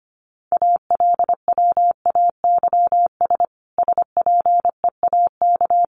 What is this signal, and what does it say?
Morse code, heard as a keyed tone, about 25 wpm, 700 Hz. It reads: ALWAYS SPEAK
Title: Sentences from Top 1100 Words 25wpm